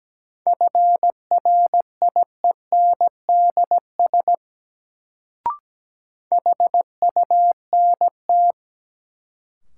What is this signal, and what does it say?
Morse code, heard as a keyed tone, about 17 wpm, 700 Hz. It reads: FRIENDS HUNT
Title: Sentences from Top 1600 Words - New Word Drill Alt - 17wpm